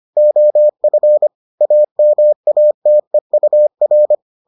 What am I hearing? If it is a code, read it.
Morse code, heard as a keyed tone, about 25 wpm, 600 Hz. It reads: OF AMATEUR